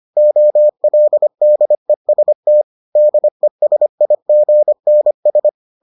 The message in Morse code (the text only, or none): OLDEST DESIGNS